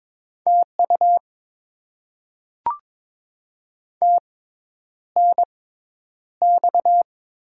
Morse code, heard as a keyed tone, about 22 wpm, spaced as for 10 wpm, 700 Hz. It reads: TU TNX
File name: Top 20 QSO Elements - 6x Character Spacing - 22wpm